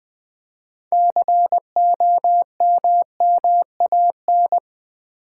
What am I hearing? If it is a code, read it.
Morse code, heard as a keyed tone, about 20 wpm, 700 Hz. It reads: COMMAN
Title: Build A Word - 3 Letters At A Time - 9-Letter Words - 20wpm